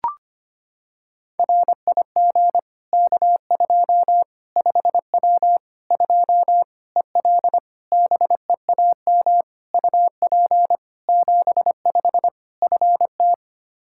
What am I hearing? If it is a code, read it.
Morse code, heard as a keyed tone, about 25 wpm, 700 Hz. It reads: RIG K2 5W 2 EL BEAM UP 75 FT
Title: Practice QSO - Exchange 2 - Info - 25wpm